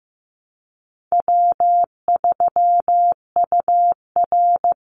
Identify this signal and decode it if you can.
Morse code, heard as a keyed tone, about 15 wpm, 700 Hz. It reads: W3UR
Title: Easy Call Signs Repeated 2x - 15wpm